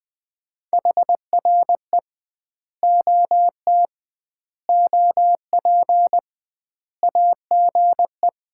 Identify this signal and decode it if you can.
Morse code, heard as a keyed tone, about 20 wpm, 700 Hz. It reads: HRE OT OP AGE
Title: Common QSO Phrases - 2x Word-Spacing - 20wpm